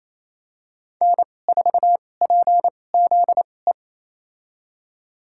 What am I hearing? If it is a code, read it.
Morse code, heard as a keyed tone, about 28 wpm, spaced as for 20 wpm, 700 Hz. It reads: N4PZE
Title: USA Call Signs - 5 Characters - 2x Character Spacing - 28wpm